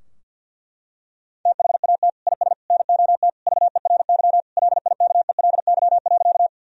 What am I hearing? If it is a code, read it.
Morse code, heard as a keyed tone, about 53 wpm, 700 Hz. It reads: THAT IS NOT VERY LIKELY.
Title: Sentences from Top 1000 Words - Review Part 4 - 50wpm